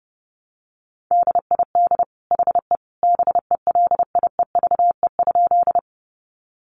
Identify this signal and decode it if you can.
Morse code, heard as a keyed tone, about 30 wpm, 700 Hz. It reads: DID HE BELIEVE?
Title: Sentences from Top 1000 Words 30wpm